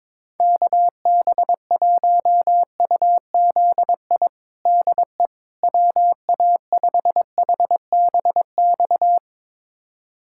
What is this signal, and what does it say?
Morse code, heard as a keyed tone, about 22 wpm, 700 Hz. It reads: KB1UZI DE WA5HBX
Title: USA Callsign DE Callsign 22wpm